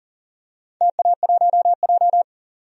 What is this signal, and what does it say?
Morse code, heard as a keyed tone, about 41 wpm, 700 Hz. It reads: TA1J
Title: DXCC-100 Call Signs - 4 Characters - 40wpm